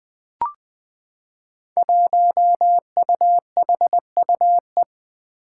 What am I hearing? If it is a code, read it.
Morse code, heard as a keyed tone, about 20 wpm, 700 Hz. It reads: 1UHUE
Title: Five Challenging Letter-Number Combinations 20wpm